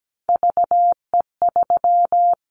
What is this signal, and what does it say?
Morse code, heard as a keyed tone, about 17 wpm, 700 Hz. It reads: VE3
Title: Callsign Prefixes for Top 25 DXCC - 17wpm